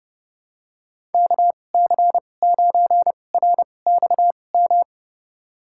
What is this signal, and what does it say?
Morse code, heard as a keyed tone, about 32 wpm, spaced as for 22 wpm, 700 Hz. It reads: KC9RXM
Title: USA Call Signs - 6 Characters - 2x Character Spacing - 30wpm